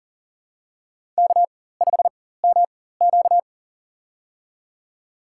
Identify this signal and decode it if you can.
Morse code, heard as a keyed tone, about 40 wpm, spaced as for 19 wpm, 700 Hz. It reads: K5MQ
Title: USA Call Signs - 4 Characters - 4x Character Spacing - 40wpm